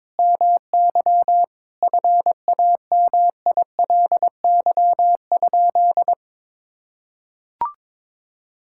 Morse code, heard as a keyed tone, about 21 wpm, 700 Hz. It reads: MY FAMILY?
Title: Sentences from Top 1000 Words - Review Part 5 - 22wpm